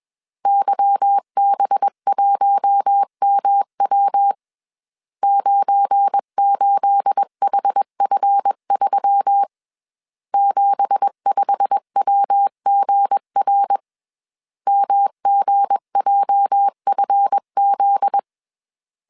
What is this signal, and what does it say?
Morse code, heard as a keyed tone, about 21 wpm, 800 Hz. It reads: YB1MW 98HF3 75WGR MGJFZ
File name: CW / DET / 1 9 1500C S C 20wpm 2310